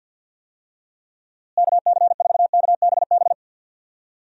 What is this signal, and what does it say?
Morse code, heard as a keyed tone, about 46 wpm, 700 Hz. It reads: KC4KBB
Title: USA Call Signs - 6 Characters - Repeated 3x - 50wpm